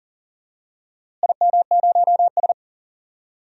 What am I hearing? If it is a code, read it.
Morse code, heard as a keyed tone, about 40 wpm, 700 Hz. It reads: IM0S
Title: DXCC-100 Call Signs - 4 Characters - Repeated 2x - 40wpm